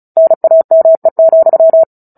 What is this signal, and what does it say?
Morse code, heard as a keyed tone, about 35 wpm, 650 Hz. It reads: NAME,
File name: Book Two / 09-Chapter 9 - 35 WPM